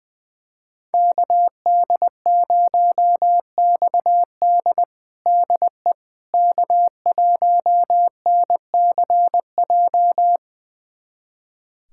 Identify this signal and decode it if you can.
Morse code, heard as a keyed tone, about 21 wpm, 700 Hz. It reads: KD0XD DE K1NCJ
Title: USA Callsign DE Callsign - 5 Characters - 20wpm